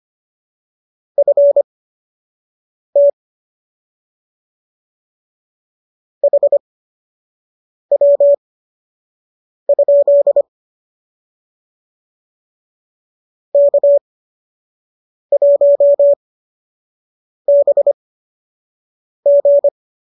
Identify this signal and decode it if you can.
Morse code, heard as a keyed tone, about 25 wpm, spaced as for 6 wpm, 590 Hz. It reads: FT HW? K1BG